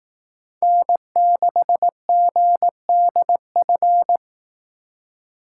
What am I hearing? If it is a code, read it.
Morse code, heard as a keyed tone, about 18 wpm, 700 Hz. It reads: N6GDF
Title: Speed Racing - USA 1x3 Call Signs 15wpm